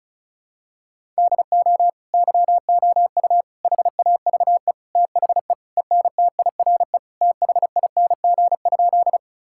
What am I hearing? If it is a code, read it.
Morse code, heard as a keyed tone, about 36 wpm, 700 Hz. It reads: DO YOU HAVE THE ENTIRE THING?
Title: Sentences from Top 1000 Words - Review Part 5 - 35wpm